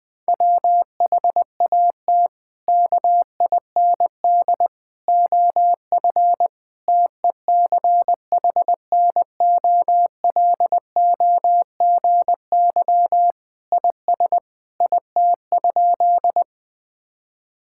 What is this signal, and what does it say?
Morse code, heard as a keyed tone, about 21 wpm, 700 Hz. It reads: WHAT KIND OF TECHNOLOGY IS IT?
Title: Sentences from Top 1000 Words - Review Part 5 - 20wpm